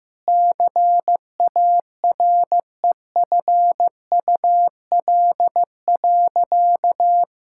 Morse code, heard as a keyed tone, about 15 wpm, 700 Hz. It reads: CAREFUL.
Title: Sentences from Top 1600 Words 15wpm